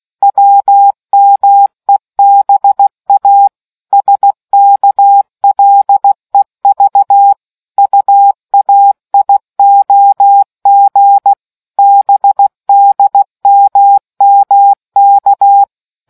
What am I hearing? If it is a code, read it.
Morse code, heard as a keyed tone, about 16 wpm, 800 Hz. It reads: WMEBA SKLEV UAIOG BDMMK